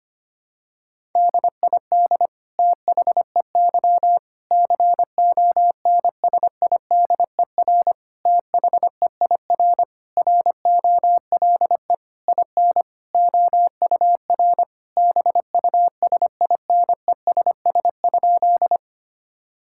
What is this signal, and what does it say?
Morse code, heard as a keyed tone, about 25 wpm, 700 Hz. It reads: DID THEY CONSIDER THEIR ROLE IN OUR BUSINESS?
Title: Sentences from Top 1000 Words - Review Part 6 - 25wpm